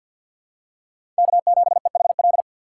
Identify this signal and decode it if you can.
Morse code, heard as a keyed tone, about 52 wpm, 700 Hz. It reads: K7EHL
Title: USA Call Signs - 4x Character Spacing - 50wpm